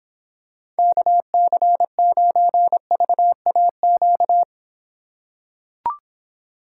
Morse code, heard as a keyed tone, about 26 wpm, 700 Hz. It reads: KC9VAQ